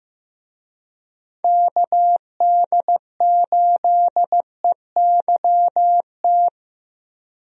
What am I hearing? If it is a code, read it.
Morse code, heard as a keyed tone, about 15 wpm, 700 Hz. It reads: KD8EYT
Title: USA Call Signs - 6 Characters - Repeated 3x - 15wpm